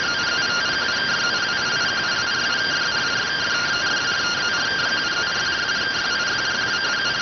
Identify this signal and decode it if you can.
Two-tone teleprinter signal, about 170 Hz shift.